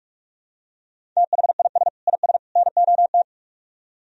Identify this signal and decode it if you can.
Morse code, heard as a keyed tone, about 47 wpm, 700 Hz. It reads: THIS IS NOT